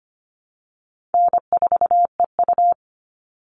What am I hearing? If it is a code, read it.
Morse code, heard as a keyed tone, about 25 wpm, 700 Hz. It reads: N4EU
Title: All Cut Numbers - 4-Digit 25wpm